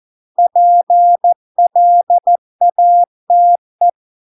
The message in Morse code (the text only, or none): PLATE